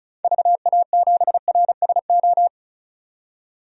Morse code, heard as a keyed tone, about 36 wpm, 700 Hz. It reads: VA7RSO
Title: DXCC-100 Call Signs - 6 Characters - 35wpm